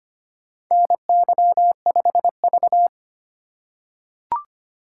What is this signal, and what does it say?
Morse code, heard as a keyed tone, about 25 wpm, 700 Hz. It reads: NY5V
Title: USA Call Signs - 4 Characters - 3x Character Spacing - 25wpm